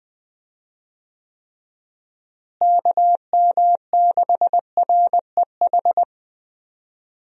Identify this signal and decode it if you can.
Morse code, heard as a keyed tone, about 20 wpm, 700 Hz. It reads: KM6REH
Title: USA Call Signs - 6 Characters - Repeated 2x - 20wpm